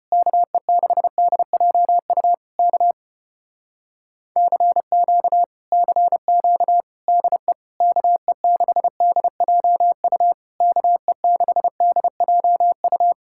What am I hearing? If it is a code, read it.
Morse code, heard as a keyed tone, about 33 wpm, 700 Hz. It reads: KE6DJU K CQ CQ DE KE6DJU KE6DJU